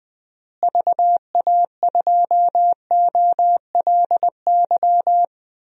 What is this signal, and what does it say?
Morse code, heard as a keyed tone, about 21 wpm, 700 Hz. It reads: VA2OLY
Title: DXCC-100 Call Signs - 6 Characters - Repeated 2x - 20wpm